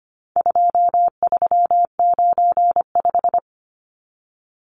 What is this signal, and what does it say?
Morse code, heard as a keyed tone, about 25 wpm, 700 Hz. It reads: 2395
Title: Numbers - 4-Digit 25wpm